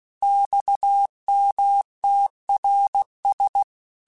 Morse code, heard as a keyed tone, about 16 wpm, 800 Hz. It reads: XMTRS